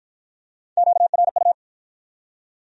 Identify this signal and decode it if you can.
Morse code, heard as a keyed tone, about 52 wpm, 700 Hz. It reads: QRU